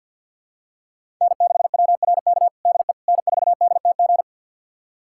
Morse code, heard as a keyed tone, about 49 wpm, 700 Hz. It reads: N6WRK DE N4DTG
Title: USA Callsign DE Callsign 50wpm